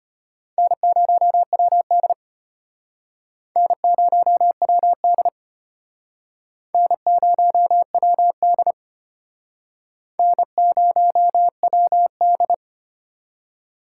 Morse code, heard as a keyed tone, about 31 wpm, 700 Hz. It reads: N0WD N0WD N0WD N0WD